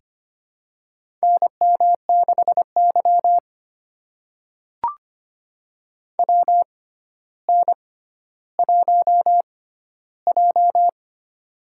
Morse code, heard as a keyed tone, about 25 wpm, spaced as for 13 wpm, 700 Hz. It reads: NM6Y WN1J
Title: USA Call Signs - 4 Characters - 6x Character Spacing - 25wpm